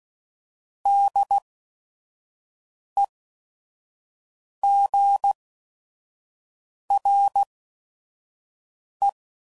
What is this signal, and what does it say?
Morse code, heard as a keyed tone, about 16 wpm, spaced as for 5 wpm, 800 Hz. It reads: DEGRE
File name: CW Code Groups @5WPM / Weather Sets / Weather 4 17